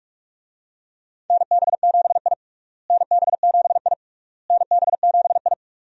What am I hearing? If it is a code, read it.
Morse code, heard as a keyed tone, about 41 wpm, 700 Hz. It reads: NB7I NB7I NB7I
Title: USA Call Signs - 4 Characters - Repeated 3x - 45wpm